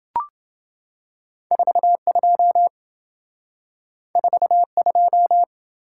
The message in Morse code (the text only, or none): 42 42